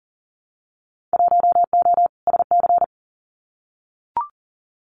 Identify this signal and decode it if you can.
Morse code, heard as a keyed tone, about 41 wpm, 700 Hz. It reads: 1O SC
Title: Winter Field Day Exchanges 40wpm